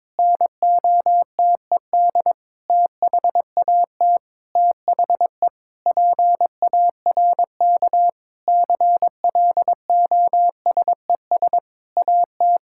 Morse code, heard as a keyed tone, about 22 wpm, 700 Hz. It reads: NOTED THAT THE PARK CLOSES AT
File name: Sentences with 7-Word Phrases - 22wpm